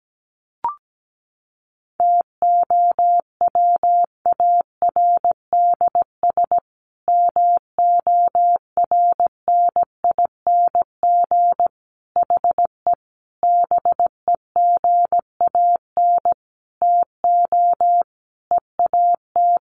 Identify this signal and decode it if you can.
Morse code, heard as a keyed tone, about 17 wpm, 700 Hz. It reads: TOWARDS MORNING HE BEGAN TO EAT
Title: Sentences from Top 1100 Words 17wpm